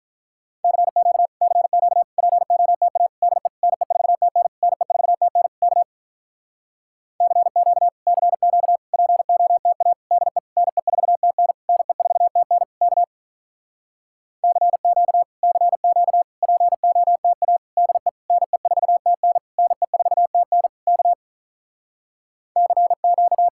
Morse code, heard as a keyed tone, about 47 wpm, 700 Hz. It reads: CQ CQ POTA DE NE4TN NE4TN K CQ CQ POTA DE NE4TN NE4TN K CQ CQ POTA DE NE4TN NE4TN K CQ